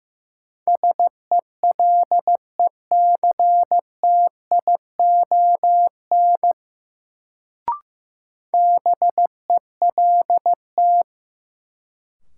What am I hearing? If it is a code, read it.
Morse code, heard as a keyed tone, about 15 wpm, 700 Hz. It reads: SELECTION BELT
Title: Sentences from Top 1400 Words - New Word Drill Alt - 15wpm